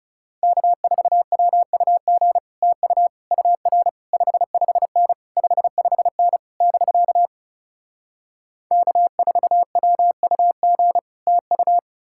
Morse code, heard as a keyed tone, about 37 wpm, 700 Hz. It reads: K4WUG TU UR 55N 55N <BK> K4WUG TU
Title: Speed-Racing Common QSO Phrases - 30wpm